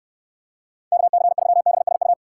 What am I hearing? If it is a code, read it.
Morse code, heard as a keyed tone, about 69 wpm, 700 Hz. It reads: KC3BSU